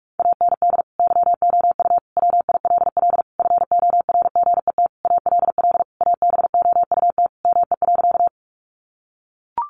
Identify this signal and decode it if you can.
Morse code, heard as a keyed tone, about 44 wpm, 700 Hz. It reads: AND YOU WILL FORGET ALL ABOUT ME.
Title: Sentences from Top 1200 Words 45wpm